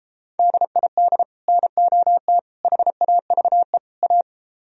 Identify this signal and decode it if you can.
Morse code, heard as a keyed tone, about 32 wpm, 700 Hz. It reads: DID NOT HAVE A